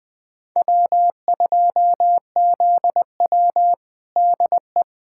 Morse code, heard as a keyed tone, about 20 wpm, 700 Hz. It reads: W2ZW DE
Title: USA Callsign DE Callsign - 4 Characters - 20wpm